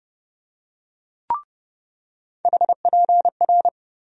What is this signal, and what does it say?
Morse code, heard as a keyed tone, about 32 wpm, 700 Hz. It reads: HPR
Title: USA Call Sign Postfixes - 3 Letters 30wpm